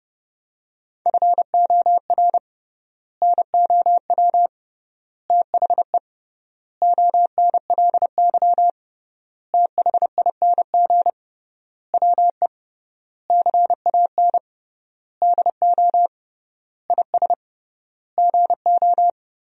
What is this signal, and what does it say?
Morse code, heard as a keyed tone, about 30 wpm, 700 Hz. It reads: FOR NOW THE ONLY THING WE CAN DO IS GO
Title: 3x Word-Spacing - Sentences from Top 200 Words - 30wpm